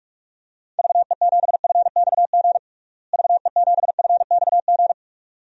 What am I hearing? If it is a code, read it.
Morse code, heard as a keyed tone, about 41 wpm, 700 Hz. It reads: VE7FXG VE7FXG
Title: DXCC-100 Call Signs - 6 Characters - Repeated 2x - 45wpm